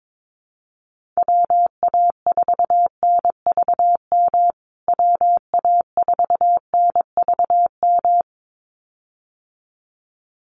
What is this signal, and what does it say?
Morse code, heard as a keyed tone, about 22 wpm, 700 Hz. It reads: WA4NVM WA4NVM
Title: USA Call Signs Repeated 2x 22wpm